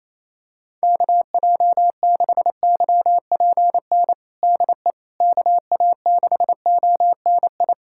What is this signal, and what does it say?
Morse code, heard as a keyed tone, about 28 wpm, 700 Hz. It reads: KJ6YPN DE KA6ONI
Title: USA Callsign DE Callsign - 6 Characters - 28wpm